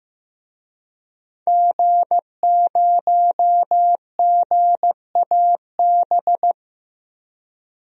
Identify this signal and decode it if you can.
Morse code, heard as a keyed tone, about 15 wpm, 700 Hz. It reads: G0GAB
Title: DXCC-100 Call Signs - 5 Characters - 15wpm